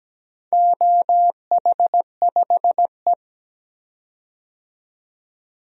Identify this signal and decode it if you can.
Morse code, heard as a keyed tone, about 17 wpm, 700 Hz. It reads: OH5E
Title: DXCC-100 Call Signs - 4 Characters - 17wpm